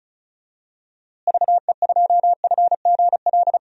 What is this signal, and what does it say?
Morse code, heard as a keyed tone, about 35 wpm, 700 Hz. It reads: VE2FGL
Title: DXCC-100 Call Signs - 6 Characters - Repeated 3x - 35wpm